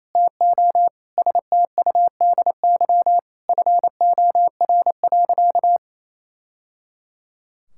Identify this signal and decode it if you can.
Morse code, heard as a keyed tone, about 28 wpm, 700 Hz. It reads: TO STUDY FOR.
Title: Sentences from Top 1000 Words - Review Part 5 - 28wpm